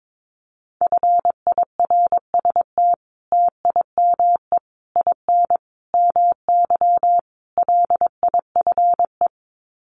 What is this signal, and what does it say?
Morse code, heard as a keyed tone, about 22 wpm, 700 Hz. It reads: FIRST TIME IN MY LIFE
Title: Sets of 5 Words Encore2 22wpm